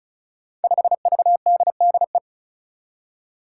Morse code, heard as a keyed tone, about 36 wpm, 700 Hz. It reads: 5VDDE
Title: Five Challenging Letter-Number Combinations 35wpm